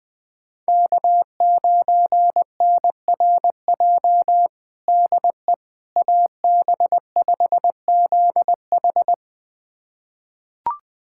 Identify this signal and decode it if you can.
Morse code, heard as a keyed tone, about 21 wpm, 700 Hz. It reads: K9NRJ DE AB5ZH
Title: USA Callsign DE Callsign - 5 Characters - 20wpm